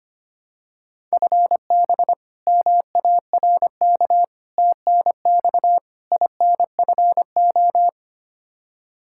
Morse code, heard as a keyed tone, about 25 wpm, 700 Hz. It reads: FB MARK TNX INFO
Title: K1USN SST QSO Practice 25wpm